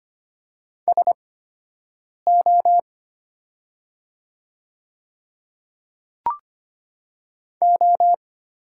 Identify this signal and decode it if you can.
Morse code, heard as a keyed tone, about 25 wpm, spaced as for 7 wpm, 700 Hz. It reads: SO O
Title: 8x Character Spacing - Top 100 Words - 25wpm